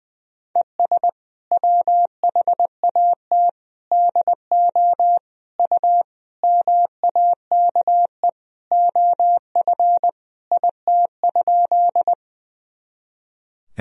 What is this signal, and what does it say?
Morse code, heard as a keyed tone, about 20 wpm, 700 Hz. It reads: ES WHAT DO U MAKE OF IT?
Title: Sentences from Top 100 Words and Abbreviations 20wpm